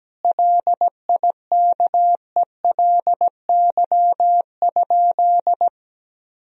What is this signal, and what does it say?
Morse code, heard as a keyed tone, about 17 wpm, 700 Hz. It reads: LIKELY?
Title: Sentences from Top 1000 Words - Review Part 4 - 17wpm